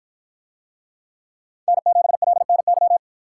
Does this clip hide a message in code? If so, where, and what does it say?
Yes: Morse code from 1.7 to 3.0 s, heard as a keyed tone, about 49 wpm, 700 Hz. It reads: N7LNY